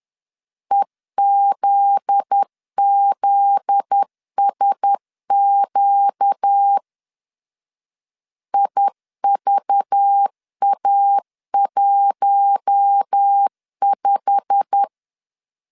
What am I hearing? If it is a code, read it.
Morse code, heard as a keyed tone, about 10 wpm, 800 Hz. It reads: EZZSQ IVA15
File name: CW / DET / 1 9 1500C S A 10wpm 4511